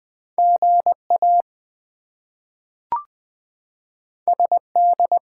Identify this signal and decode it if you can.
Morse code, heard as a keyed tone, about 20 wpm, 700 Hz. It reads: GA SD
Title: US States and Canadian Provinces 20wpm